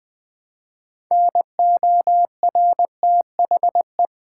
Morse code, heard as a keyed tone, about 20 wpm, 700 Hz. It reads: NORTHE
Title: Build A Word - 3 Letters At A Time - 8-Letter Words - 20wpm